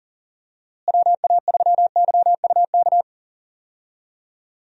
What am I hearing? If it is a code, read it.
Morse code, heard as a keyed tone, about 40 wpm, 700 Hz. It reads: WA3YUK
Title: USA Call Signs - 6 Characters - Repeated 2x - 40wpm